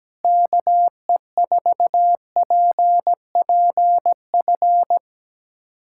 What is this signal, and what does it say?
Morse code, heard as a keyed tone, about 17 wpm, 700 Hz. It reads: KE4PPF
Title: USA Callsign DE Callsign 17wpm